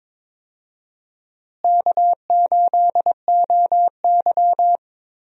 Morse code, heard as a keyed tone, about 21 wpm, 700 Hz. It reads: K8OY
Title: USA Call Signs - 4 Characters - Repeated 3x - 22wpm